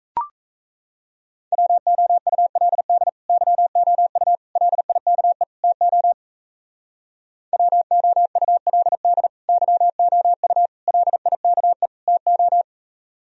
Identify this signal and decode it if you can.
Morse code, heard as a keyed tone, about 35 wpm, 700 Hz. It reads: WOULD YOU LIKE TO WOULD YOU LIKE TO